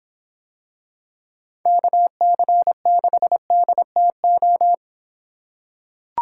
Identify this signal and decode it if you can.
Morse code, heard as a keyed tone, about 26 wpm, 700 Hz. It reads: KC6DTO